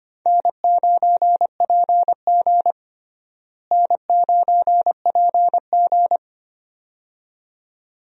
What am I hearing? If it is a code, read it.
Morse code, heard as a keyed tone, about 25 wpm, 700 Hz. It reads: N9PG N9PG
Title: USA Call Signs - 4 Characters - Repeated 2x - 25wpm